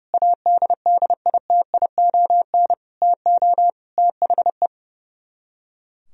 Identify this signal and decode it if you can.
Morse code, heard as a keyed tone, about 33 wpm, 700 Hz. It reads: ADDITION TO THE
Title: Speed-Racing - Sets of 4 Words 30wpm